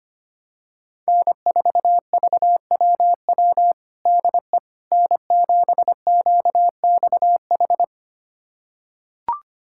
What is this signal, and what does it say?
Morse code, heard as a keyed tone, about 25 wpm, 700 Hz. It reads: N4VWW DE N7QXH
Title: USA Callsign DE Callsign - 5 Characters - 25wpm